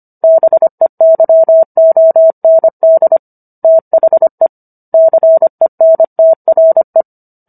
Morse code, heard as a keyed tone, about 25 wpm, 650 Hz. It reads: BEYOND THE CENTRE